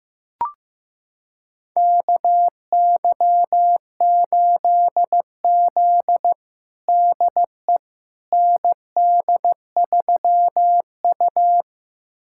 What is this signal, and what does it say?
Morse code, heard as a keyed tone, about 15 wpm, 700 Hz. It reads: KY8Z DE ND3U
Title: USA Callsign DE Callsign - 4 Characters - 15wpm